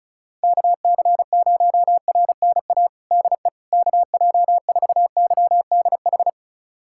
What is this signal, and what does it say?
Morse code, heard as a keyed tone, about 36 wpm, 700 Hz. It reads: KC0RNA DE KJ4YDH
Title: USA Callsign DE Callsign 35wpm